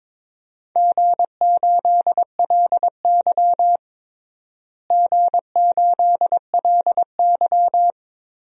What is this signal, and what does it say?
Morse code, heard as a keyed tone, about 22 wpm, 700 Hz. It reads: G8LY G8LY
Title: DXCC-100 Call Signs - 4 Characters - Repeated 3x - 22wpm